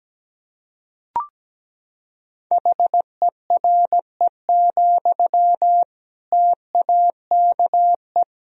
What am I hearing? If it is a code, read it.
Morse code, heard as a keyed tone, about 17 wpm, 700 Hz. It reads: HERE, TAKE
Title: Sentences from Top 1000 Words - Review Part 1 - 17wpm